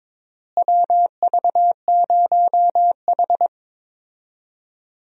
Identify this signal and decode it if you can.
Morse code, heard as a keyed tone, about 22 wpm, 700 Hz. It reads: WV0H
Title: USA Call Signs - 4 Characters - Repeated 3x - 22wpm